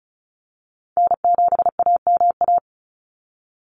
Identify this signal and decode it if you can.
Morse code, heard as a keyed tone, about 34 wpm, 700 Hz. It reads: N7AMA
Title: NA SOTA Chaser Callsigns 35wpm